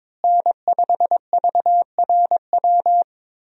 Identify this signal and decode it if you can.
Morse code, heard as a keyed tone, about 22 wpm, 700 Hz. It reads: N5VRW
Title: USA Call Signs - 5 Characters - Repeated 3x - 22wpm